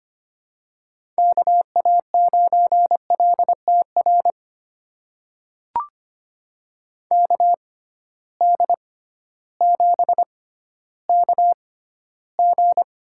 Morse code, heard as a keyed tone, about 25 wpm, spaced as for 14 wpm, 700 Hz. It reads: KA9LTR KD7KG
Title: USA Call Signs - 6 Characters - 6x Character Spacing - 25wpm